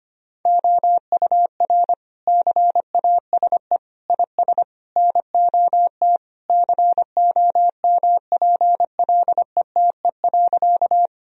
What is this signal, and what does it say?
Morse code, heard as a keyed tone, about 25 wpm, 700 Hz. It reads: OUR CASE IS NOT COMPLETE.
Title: Sentences from Top 1000 Words - Review Part 2 - 25wpm